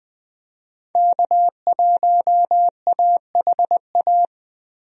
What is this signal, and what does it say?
Morse code, heard as a keyed tone, about 21 wpm, 700 Hz. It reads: K1AHA
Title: USA Call Signs - 5 Characters - Repeated 3x - 20wpm